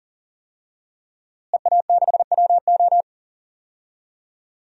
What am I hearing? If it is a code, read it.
Morse code, heard as a keyed tone, about 40 wpm, 700 Hz. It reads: EA6WO
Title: DXCC-100 Call Signs - 5 Characters - Repeated 3x - 40wpm